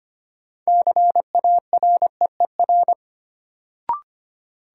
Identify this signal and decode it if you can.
Morse code, heard as a keyed tone, about 25 wpm, 700 Hz. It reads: CAREER
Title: Sentences from Top 1000 Words - New Word Drill - 25wpm